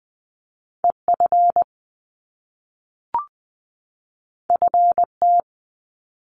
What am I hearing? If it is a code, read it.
Morse code, heard as a keyed tone, about 20 wpm, 700 Hz. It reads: E F F T
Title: Top Two Letter Combinations in Words 20wpm